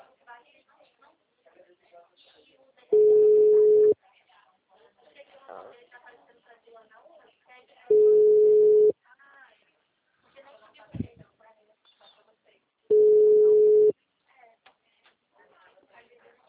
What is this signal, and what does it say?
Ringback tone (Europe, 425 Hz), with speech.